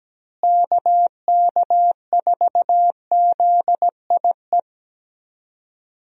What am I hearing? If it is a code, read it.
Morse code, heard as a keyed tone, about 17 wpm, 700 Hz. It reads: KK4ZIE
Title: USA Call Signs - 6 Characters - Repeated 3x - 17wpm